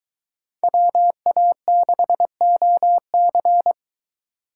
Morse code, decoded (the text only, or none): WA6OC